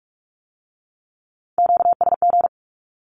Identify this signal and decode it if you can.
Morse code, heard as a keyed tone, about 46 wpm, 700 Hz. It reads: QSG
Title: Q-Codes with Spoken Meaning - 45wpm